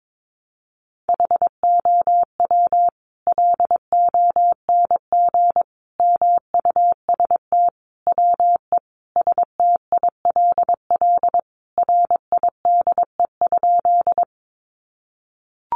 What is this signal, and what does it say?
Morse code, heard as a keyed tone, about 22 wpm, 700 Hz. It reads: HOW LONG MUST WE STILL RIDE?
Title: Sentences from Top 1200 Words 22wpm